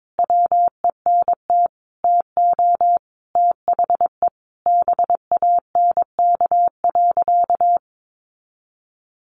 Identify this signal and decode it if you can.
Morse code, heard as a keyed tone, about 22 wpm, 700 Hz. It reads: WENT TO THE BANK.
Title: Sentences from Top 1300 Words 22wpm